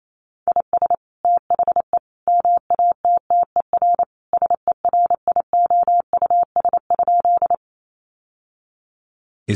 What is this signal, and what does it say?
Morse code, heard as a keyed tone, about 28 wpm, 700 Hz. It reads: IS THE MATTER SERIOUS?
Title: Sentences from Top 1100 Words 28wpm